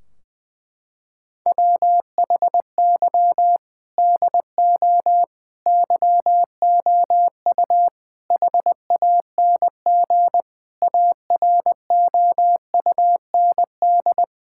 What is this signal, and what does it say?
Morse code, heard as a keyed tone, about 21 wpm, 700 Hz. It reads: WHY DO YOU HANG AROUND
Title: Sentences from Top 1600 Words 20wpm